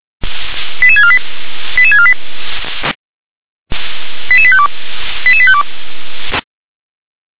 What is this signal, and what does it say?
Five-tone selective call (ZVEI), heard as quick runs of five tones, tones of about 70 ms. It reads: 88648 88648 88852 88852
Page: /357/SDR/ZVEI/88085/2022.10.18/